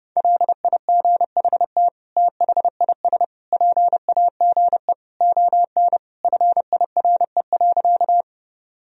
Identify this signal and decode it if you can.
Morse code, heard as a keyed tone, about 30 wpm, 700 Hz. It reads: LIGHT THIS PAGE ON FIRE.
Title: Sentences from Top 1000 Words - Review Part 1 - 30wpm